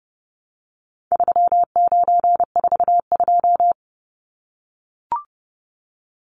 Morse code, heard as a keyed tone, about 30 wpm, 700 Hz. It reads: 3942
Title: Numbers - 4-Digit 30wpm